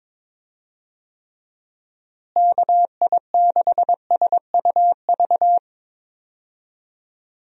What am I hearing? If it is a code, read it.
Morse code, heard as a keyed tone, about 22 wpm, 700 Hz. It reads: KI6SUV
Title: USA Call Signs - 6 Characters - Repeated 2x - 22wpm